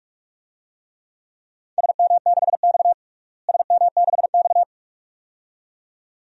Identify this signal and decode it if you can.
Morse code, heard as a keyed tone, about 47 wpm, 700 Hz. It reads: SM6X SM6X
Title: DXCC-100 Call Signs - 4 Characters - Repeated 2x - 45wpm